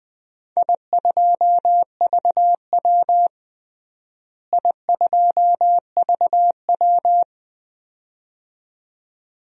Morse code, heard as a keyed tone, about 21 wpm, 700 Hz. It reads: I2VW I2VW
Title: DXCC-100 Call Signs - 4 Characters - Repeated 2x - 20wpm